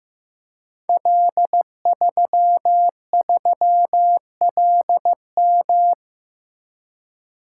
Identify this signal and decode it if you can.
Morse code, heard as a keyed tone, about 15 wpm, 700 Hz. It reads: L33LM
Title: Code Groups - 5 Letter:Numbers - 15wpm